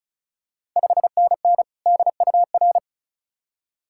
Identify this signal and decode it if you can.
Morse code, heard as a keyed tone, about 36 wpm, 690 Hz. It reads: 5NN DUR
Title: Ontario QSO Party Exchange 35wpm